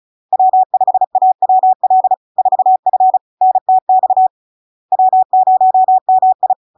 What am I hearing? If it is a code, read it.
Morse code, heard as a keyed tone, about 35 wpm, 750 Hz. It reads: W5AWL 4F NTX W0MI
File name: LARK-FD-2022-Log-in-CW-35wpm-750-Hz